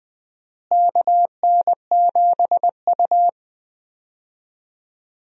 Morse code, heard as a keyed tone, about 20 wpm, 700 Hz. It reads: KN7U
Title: USA Call Signs - 4 Characters - Repeated 3x - 20wpm